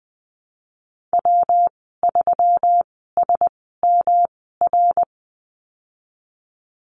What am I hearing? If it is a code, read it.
Morse code, heard as a keyed tone, about 19 wpm, spaced as for 14 wpm, 700 Hz. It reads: W3SMR
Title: 122 Lesson 20g - Number 3 - Call Signs 20wpm